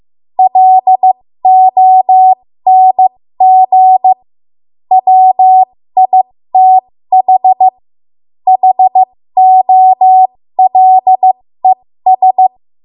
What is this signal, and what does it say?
Morse code, heard as a keyed tone, about 15 wpm, spaced as for 13 wpm, 750 Hz. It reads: LONG WITH HOLES
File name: Morse / 13WPM